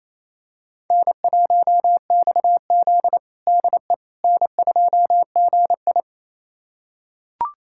Morse code, heard as a keyed tone, about 28 wpm, 700 Hz. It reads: N1XZ DE N2GI
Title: USA Callsign DE Callsign - 4 Characters - 28wpm